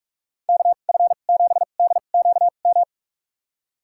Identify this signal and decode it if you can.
Morse code, heard as a keyed tone, about 46 wpm, spaced as for 33 wpm, 700 Hz. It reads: KF7DQM
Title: USA Call Signs - 2x Character Spacing - 45wpm